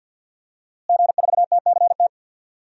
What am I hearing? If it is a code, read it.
Morse code, heard as a keyed tone, about 50 wpm, 700 Hz. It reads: G4TCT
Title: DXCC-100 Call Signs - 5 Characters - Repeated 2x - 50wpm